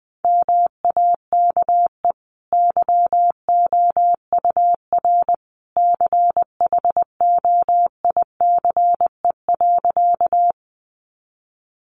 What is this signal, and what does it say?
Morse code, heard as a keyed tone, about 20 wpm, 700 Hz. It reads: MAKE YOUR CHOICE.
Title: Sentences from Top 900 Words 20wpm